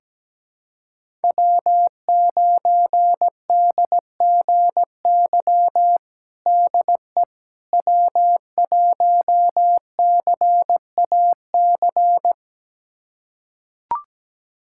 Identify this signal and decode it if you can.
Morse code, heard as a keyed tone, about 17 wpm, 700 Hz. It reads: W9DGY DE W1CAC
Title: USA Callsign DE Callsign - 5 Characters - 17wpm